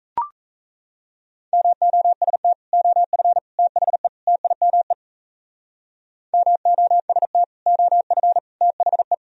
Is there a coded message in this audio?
Yes: Morse code from 1.5 to 9.1 s, heard as a keyed tone, about 36 wpm, 700 Hz. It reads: MOST OF THE TIME MOST OF THE